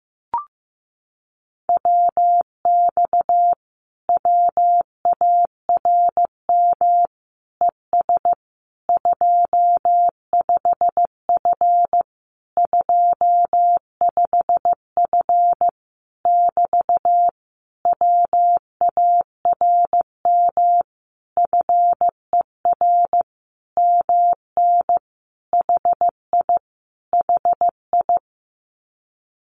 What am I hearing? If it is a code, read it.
Morse code, heard as a keyed tone, about 14 wpm, 700 Hz. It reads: WX WARM ES 25F 25F <BT> WARM FER MN HI HI
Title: Common QSO Phrases Encore 15wpm